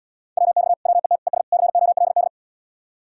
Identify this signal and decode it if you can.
Morse code, heard as a keyed tone, about 78 wpm, 700 Hz. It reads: WX WET ES COLD